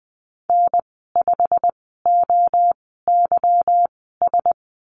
Morse code, heard as a keyed tone, about 20 wpm, spaced as for 14 wpm, 700 Hz. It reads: N5OYS
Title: 161 Lesson 26g - Letter g - Call Signs 20wpm